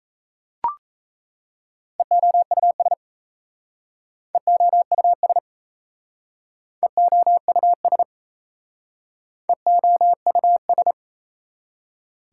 Morse code, heard as a keyed tone, about 38 wpm, 700 Hz. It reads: EOUS EOUS EOUS EOUS